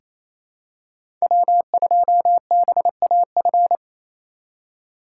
Morse code, heard as a keyed tone, about 28 wpm, 700 Hz. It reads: W2BAF
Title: USA Call Signs - 5 Characters - 2x Character Spacing - 28wpm